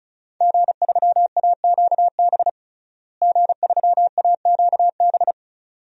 Morse code, heard as a keyed tone, about 36 wpm, 700 Hz. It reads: G3AQB G3AQB
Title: DXCC-100 Call Signs - 5 Characters - Repeated 2x - 35wpm